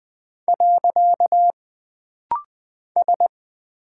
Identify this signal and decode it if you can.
Morse code, heard as a keyed tone, about 20 wpm, 700 Hz. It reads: . S
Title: Single Character (Letter:Number:Punctuation:Prosign) - Rapid-Fire - 20wpm